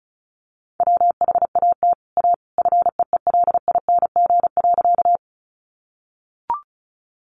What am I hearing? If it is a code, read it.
Morse code, heard as a keyed tone, about 34 wpm, 700 Hz. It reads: WHAT A FEELING.
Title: Sentences from Top 1200 Words 35wpm